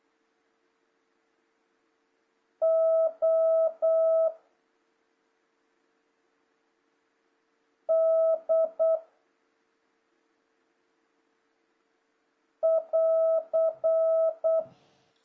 Morse code, heard as a keyed tone, about 8 wpm, 650 Hz. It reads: O D <AR>
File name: jh8cbh / 2026cw